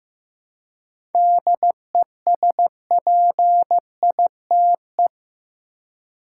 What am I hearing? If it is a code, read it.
Morse code, heard as a keyed tone, about 15 wpm, 700 Hz. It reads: DESPITE
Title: Seven Letter Words - 15wpm